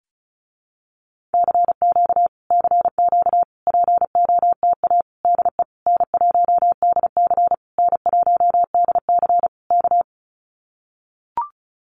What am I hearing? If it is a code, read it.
Morse code, heard as a keyed tone, about 34 wpm, 700 Hz. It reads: CQ CQ POTA DE N1DC N1DC K
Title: Common QSO Phrases Encore 35wpm